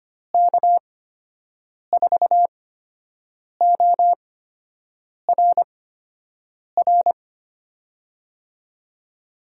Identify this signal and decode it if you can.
Morse code, heard as a keyed tone, about 25 wpm, spaced as for 7 wpm, 700 Hz. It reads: K4ORR
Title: USA Call Signs - 6 Characters - 8x Character Spacing - 25wpm